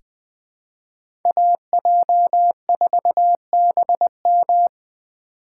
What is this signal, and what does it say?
Morse code, heard as a keyed tone, about 20 wpm, 700 Hz. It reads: AJ4BM
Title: USA Call Signs - 5 Characters - 2x Character Spacing - 20wpm